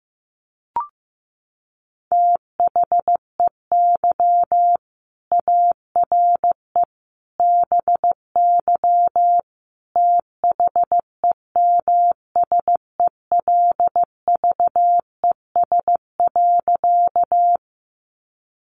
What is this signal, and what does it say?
Morse code, heard as a keyed tone, about 15 wpm, 700 Hz. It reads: THEY ARE BY THEMSELVES.
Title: Sentences from Top 800 Words 15wpm